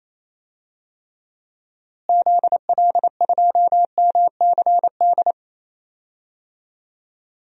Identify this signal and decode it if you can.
Morse code, heard as a keyed tone, about 28 wpm, 700 Hz. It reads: ZL2MCD
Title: DXCC-100 Call Signs - 6 Characters - Repeated 2x - 28wpm